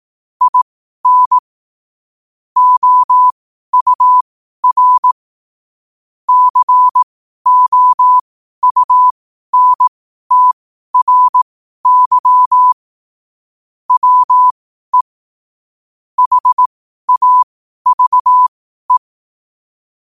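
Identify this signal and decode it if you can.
Morse code, heard as a keyed tone, about 18 wpm, spaced as for 12 wpm, 1000 Hz. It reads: IN OUR COUNTRY WE HAVE
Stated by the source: 12 wpm Morse code.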